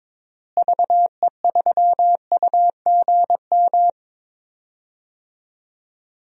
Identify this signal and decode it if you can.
Morse code, heard as a keyed tone, about 22 wpm, 700 Hz. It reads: VE3UGM
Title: DXCC-100 Call Signs - 6 Characters - Repeated 2x - 22wpm